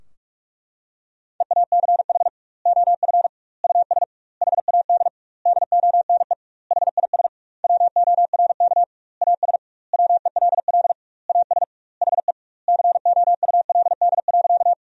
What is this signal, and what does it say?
Morse code, heard as a keyed tone, about 46 wpm, 700 Hz. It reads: EACH OF US HAD DONE HIS WORK AS WELL AS HE COULD.
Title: 2x Word-Spacing - Sentences from Top 500 Words - 45wpm